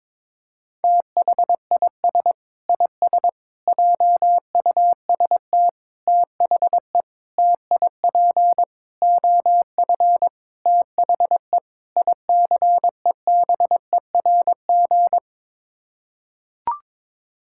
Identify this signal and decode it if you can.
Morse code, heard as a keyed tone, about 22 wpm, 700 Hz. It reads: THIS IS JUST THE TIP OF THE ICEBERG
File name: practice / 22wpm / Sets of 8 Words - 22wpm